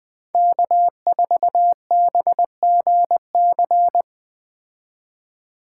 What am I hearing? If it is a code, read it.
Morse code, heard as a keyed tone, about 20 wpm, 700 Hz. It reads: K4BGC